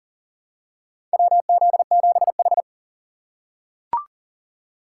Morse code, heard as a keyed tone, about 40 wpm, 700 Hz. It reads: WZ7H
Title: USA Call Signs - 2x Character Spacing - 40wpm